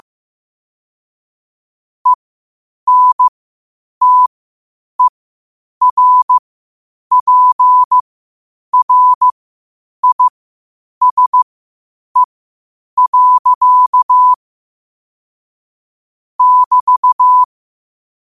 Morse code, heard as a keyed tone, about 15 wpm, spaced as for 8 wpm, 1000 Hz. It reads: ENTERPRISE. <BT>
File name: qotdfiles / 8wpm